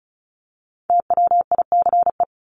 Morse code, heard as a keyed tone, about 34 wpm, 700 Hz. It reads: TWICE
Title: Top 1501-1750 Words 35wpm